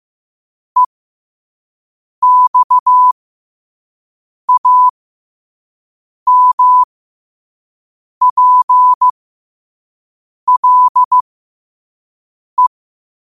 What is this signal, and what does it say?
Morse code, heard as a keyed tone, about 15 wpm, spaced as for 5 wpm, 1000 Hz. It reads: EXAMPLE
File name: qotdfiles / 5wpm